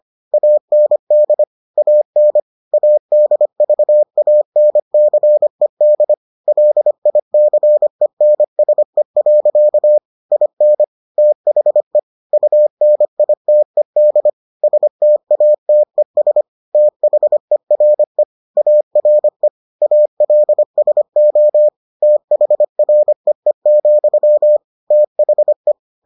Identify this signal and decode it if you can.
Morse code, heard as a keyed tone, about 25 wpm, 600 Hz. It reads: AND AN ADVANCED LICENSE. IN THE UNITED STATES THERE ARE ALSO THREE, THE